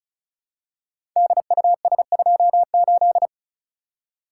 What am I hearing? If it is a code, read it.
Morse code, heard as a keyed tone, about 36 wpm, 700 Hz. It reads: DUS28
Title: Code Groups - 5 Letter:Numbers - 35wpm